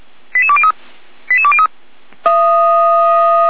Five-tone selective call (ZVEI), heard as quick runs of five tones, tones of about 70 ms. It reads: 88203 88203